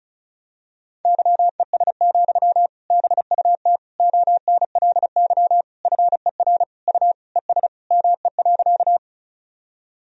Morse code, heard as a keyed tone, about 36 wpm, 700 Hz. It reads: YES, BUT ONLY FER U ES ME.
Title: Sentences from Top 500 Words and Abbreviations 35wpm